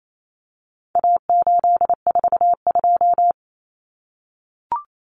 Morse code, heard as a keyed tone, about 28 wpm, 700 Hz. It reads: A842
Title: Common Cut Numbers - 4-Digit 28wpm